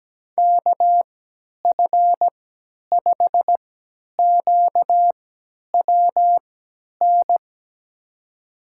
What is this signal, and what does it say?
Morse code, heard as a keyed tone, about 17 wpm, spaced as for 10 wpm, 700 Hz. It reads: KF5QWN
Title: USA Call Signs - 6 Characters - 3x Character Spacing - 17wpm